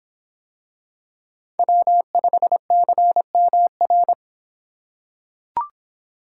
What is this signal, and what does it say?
Morse code, heard as a keyed tone, about 25 wpm, 700 Hz. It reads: W5CMR